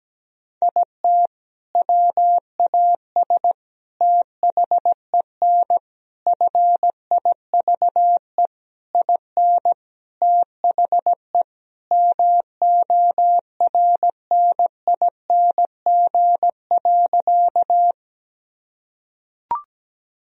Morse code, heard as a keyed tone, about 17 wpm, 700 Hz. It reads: IT WAS THEN FIVE IN THE MORNING.
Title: Sentences from Top 1000 Words - Review Part 2 - 17wpm